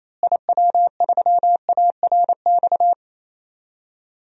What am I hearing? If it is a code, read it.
Morse code, heard as a keyed tone, about 28 wpm, 700 Hz. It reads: IW3ARX
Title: DXCC-100 Call Signs - 6 Characters - Repeated 3x - 28wpm